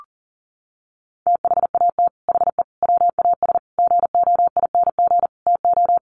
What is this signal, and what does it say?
Morse code, heard as a keyed tone, about 40 wpm, 700 Hz. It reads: THAT HE WAS GOING TO
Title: Sets of 5 Words 40wpm